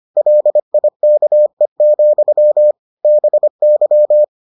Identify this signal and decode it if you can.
Morse code, heard as a keyed tone, about 25 wpm, 600 Hz. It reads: LIKE, BY